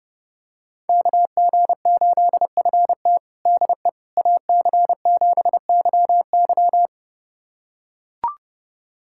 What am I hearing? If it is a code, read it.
Morse code, heard as a keyed tone, about 30 wpm, 700 Hz. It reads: KG8FT DE AC7YY
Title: USA Callsign DE Callsign - 5 Characters - 30wpm